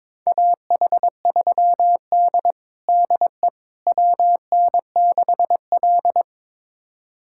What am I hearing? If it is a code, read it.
Morse code, heard as a keyed tone, about 22 wpm, 700 Hz. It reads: AH3D DE WN6L
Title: USA Callsign DE Callsign - 4 Characters - 22wpm